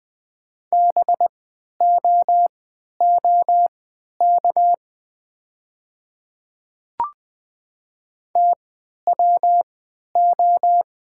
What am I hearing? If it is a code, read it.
Morse code, heard as a keyed tone, about 20 wpm, spaced as for 11 wpm, 700 Hz. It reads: BOOK TWO
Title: 3x Character Spacing - Top 300 Words - 20wpm